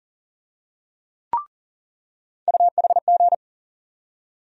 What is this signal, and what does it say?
Morse code, heard as a keyed tone, about 40 wpm, 700 Hz. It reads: UHG
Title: USA Call Sign Postfixes - 3 Letters 40wpm